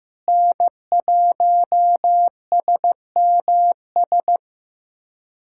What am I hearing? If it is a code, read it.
Morse code, heard as a keyed tone, about 16 wpm, 700 Hz. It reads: N1SMS
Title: USA Callsign DE Callsign - 5 Characters - 15wpm